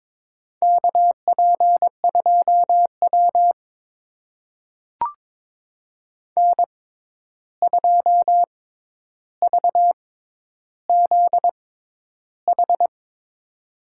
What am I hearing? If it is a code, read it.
Morse code, heard as a keyed tone, about 21 wpm, spaced as for 10 wpm, 700 Hz. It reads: KP2W N2VZH
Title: USA Call Signs - 6x Character Spacing - 22wpm